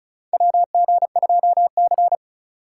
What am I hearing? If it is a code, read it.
Morse code, heard as a keyed tone, about 36 wpm, 700 Hz. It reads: WG2C
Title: USA Call Signs - 4 Characters - 8x Character Spacing - 35wpm